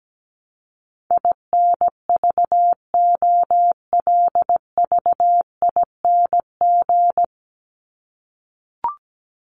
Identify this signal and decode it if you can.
Morse code, heard as a keyed tone, about 17 wpm, 700 Hz. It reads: INVOLVING
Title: ING Suffix - Top 100 Words with Base Words 17wpm